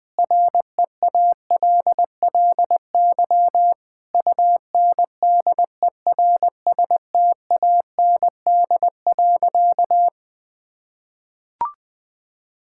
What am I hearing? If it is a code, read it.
Morse code, heard as a keyed tone, about 20 wpm, 700 Hz. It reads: REALLY UNDERSTAND.
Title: Sentences with 4-Word Phrases - 20wpm